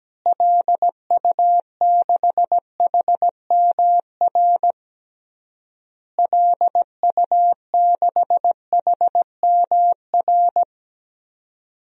Morse code, heard as a keyed tone, about 17 wpm, 700 Hz. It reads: LU6HMR LU6HMR
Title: DXCC-100 Call Signs - 6 Characters - Repeated 3x - 17wpm